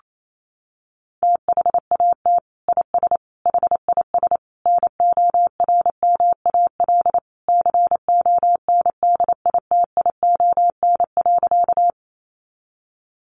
Morse code, heard as a keyed tone, about 28 wpm, 700 Hz. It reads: THAT IS HIS NORMAL CONDITION.
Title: Sentences from Top 1200 Words 28wpm